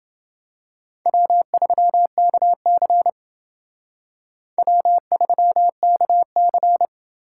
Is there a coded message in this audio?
Yes: Morse code from 1.1 to 6.9 s, heard as a keyed tone, about 29 wpm, 700 Hz. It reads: W3KC W3KC